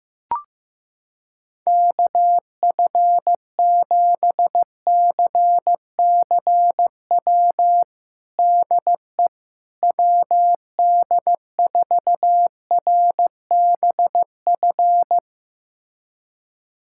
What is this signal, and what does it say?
Morse code, heard as a keyed tone, about 15 wpm, 700 Hz. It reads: KF7CCW DE WD4RBF
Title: USA Callsign DE Callsign - 6 Characters - 15wpm